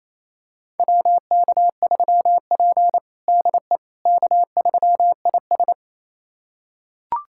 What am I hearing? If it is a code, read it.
Morse code, heard as a keyed tone, about 28 wpm, 700 Hz. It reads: WK3P DE K3IS
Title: USA Callsign DE Callsign - 4 Characters - 28wpm